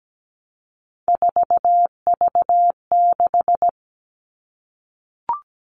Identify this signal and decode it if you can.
Morse code, heard as a keyed tone, about 17 wpm, 700 Hz. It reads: 4V6
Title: All Cut Numbers - 3-Digit 17wpm